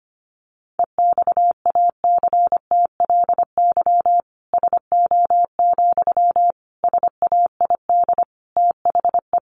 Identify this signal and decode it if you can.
Morse code, heard as a keyed tone, about 25 wpm, 700 Hz. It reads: EXACTLY SO, SAID THE
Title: Sentences from Top 900 Words 25wpm